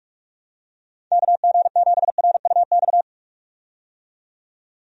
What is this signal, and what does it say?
Morse code, heard as a keyed tone, about 45 wpm, 700 Hz. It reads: KG7RUX
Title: USA Call Signs - 6 Characters - 45wpm